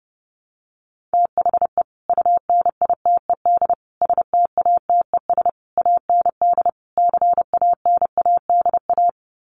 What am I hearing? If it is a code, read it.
Morse code, heard as a keyed tone, about 30 wpm, 700 Hz. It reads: THE UNITED STATES AND CANADA
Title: Sets of 5 Words 30wpm